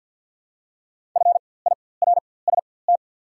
Morse code, heard as a keyed tone, about 48 wpm, spaced as for 23 wpm, 690 Hz. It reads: FIRST
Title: 4x Character Spacing - Top 500 Words - 50wpm